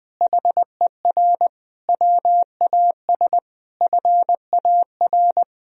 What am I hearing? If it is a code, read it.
Morse code, heard as a keyed tone, about 20 wpm, 700 Hz. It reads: HER WAS FAR
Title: Sentences from Top 1000 Words - Review Part 1 - 20wpm